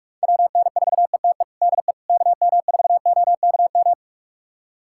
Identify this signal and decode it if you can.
Morse code, heard as a keyed tone, about 41 wpm, 700 Hz. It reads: WN4ETE DE KM4OKM
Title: USA Callsign DE Callsign 45wpm